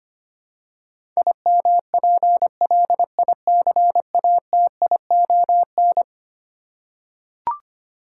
Morse code, heard as a keyed tone, about 25 wpm, 700 Hz. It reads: IMPLICATION
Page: morse-code-ninja-practice-25wpm